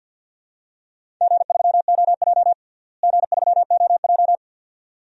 Morse code, heard as a keyed tone, about 51 wpm, 700 Hz. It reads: G3OJ G3OJ
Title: DXCC-100 Call Signs - 4 Characters - Repeated 2x - 50wpm